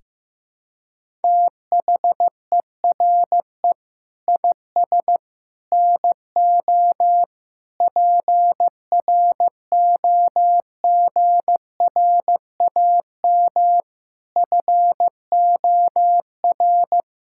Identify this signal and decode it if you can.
Morse code, heard as a keyed tone, about 15 wpm, 700 Hz. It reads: THERE IS NO PROGRAM FOR
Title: Sentences from Top 1000 Words - Review Part 5 - 15wpm